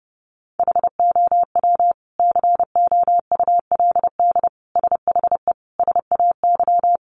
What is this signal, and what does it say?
Morse code, heard as a keyed tone, about 30 wpm, 700 Hz. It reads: HOW COULD SHE SAY
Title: Sentences from Top 200 Words 30wpm